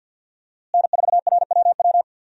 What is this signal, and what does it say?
Morse code, heard as a keyed tone, about 52 wpm, 700 Hz. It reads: N4RWW
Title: USA Call Signs - 5 Characters - Repeated 2x - 50wpm